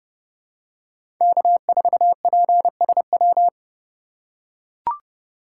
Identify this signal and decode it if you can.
Morse code, heard as a keyed tone, about 32 wpm, 700 Hz. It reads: K4PSW
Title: USA Call Signs - 5 Characters - 4x Character Spacing - 30wpm